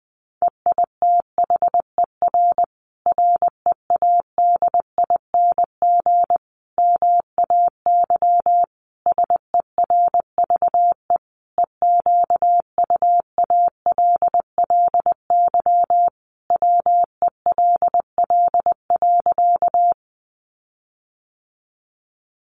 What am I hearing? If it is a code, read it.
Morse code, heard as a keyed tone, about 19 wpm, 700 Hz. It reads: EITHER READING MAY SERVE EQUALLY WELL.
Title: Sentences from Top 1400 Words 20wpm